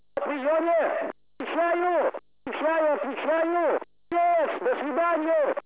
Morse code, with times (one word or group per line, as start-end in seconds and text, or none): none